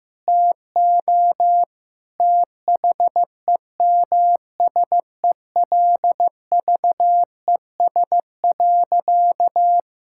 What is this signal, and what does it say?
Morse code, heard as a keyed tone, about 15 wpm, 700 Hz. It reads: TO THEMSELVES.
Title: Sentences from Top 1000 Words - Review Part 5 - 15wpm